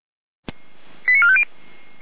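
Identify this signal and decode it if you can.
Five-tone selective call (ZVEI), heard as quick runs of five tones, tones of about 70 ms. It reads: 89450
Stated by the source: ZVEI five-tone selective call received with an SDR.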